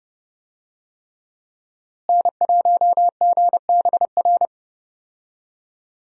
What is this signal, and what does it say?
Morse code, heard as a keyed tone, about 30 wpm, 700 Hz. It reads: N1GBR
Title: USA Call Signs - 5 Characters - Repeated 3x - 30wpm